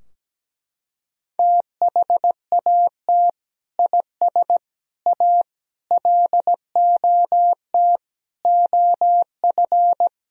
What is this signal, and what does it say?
Morse code, heard as a keyed tone, about 17 wpm, 700 Hz. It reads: THAT IS A LOT OF
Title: Sentences from Top 1000 Words - Review Part 3 - 17wpm